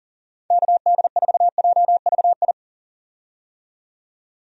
Morse code, heard as a keyed tone, about 40 wpm, 700 Hz. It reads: KD4JVI
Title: USA Call Signs - 6 Characters - 40wpm